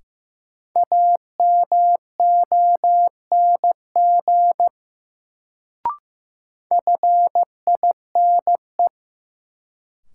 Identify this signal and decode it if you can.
Morse code, heard as a keyed tone, about 15 wpm, 700 Hz. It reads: AMONG FINE
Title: Sentences from Top 500 Words - New Word Drill - 15wpm